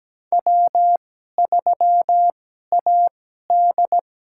Morse code, heard as a keyed tone, about 17 wpm, spaced as for 12 wpm, 700 Hz. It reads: W3AD
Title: USA Call Signs - 4 Characters - 2x Character Spacing - 17wpm